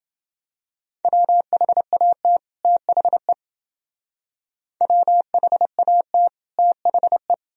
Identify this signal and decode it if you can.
Morse code, heard as a keyed tone, about 28 wpm, 700 Hz. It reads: WHAT THE WHAT THE